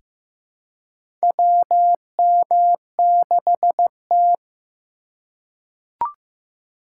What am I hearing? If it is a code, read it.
Morse code, heard as a keyed tone, about 15 wpm, 700 Hz. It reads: WM6T
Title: USA Call Signs - 4 Characters - 4x Character Spacing - 15wpm